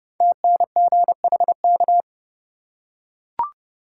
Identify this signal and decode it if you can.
Morse code, heard as a keyed tone, about 30 wpm, 700 Hz. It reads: TNGHK
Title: Code Groups - 5 Letter:Numbers - 30wpm